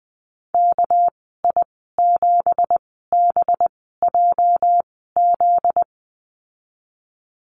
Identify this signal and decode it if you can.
Morse code, heard as a keyed tone, about 19 wpm, spaced as for 14 wpm, 700 Hz. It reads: KI7BJZ
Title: 245 Lesson 39g - Letter z - Call Signs 20wpm